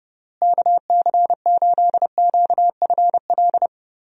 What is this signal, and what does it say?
Morse code, heard as a keyed tone, about 30 wpm, 700 Hz. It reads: KC8QFL
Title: USA Call Signs - 6 Characters - Repeated 3x - 30wpm